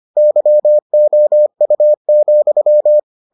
Morse code, heard as a keyed tone, about 25 wpm, 600 Hz. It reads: YOU,